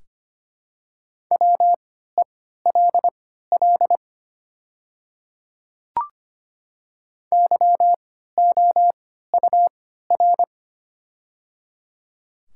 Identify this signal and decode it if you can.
Morse code, heard as a keyed tone, about 25 wpm, spaced as for 14 wpm, 700 Hz. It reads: WELL YOUR
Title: 3x Character Spacing - Top 200 Words - 25wpm